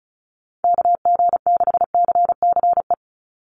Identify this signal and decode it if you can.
Morse code, heard as a keyed tone, about 34 wpm, 700 Hz. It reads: KG6CCE
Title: USA Call Signs 35wpm